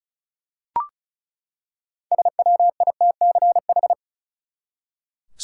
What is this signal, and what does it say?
Morse code, heard as a keyed tone, about 36 wpm, 700 Hz. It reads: SWITCH
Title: Six Letter Words - 35wpm